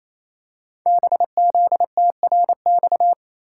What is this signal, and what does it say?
Morse code, heard as a keyed tone, about 28 wpm, 690 Hz. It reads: BZTRX
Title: Code Groups - 5 Letter:Numbers - 28wpm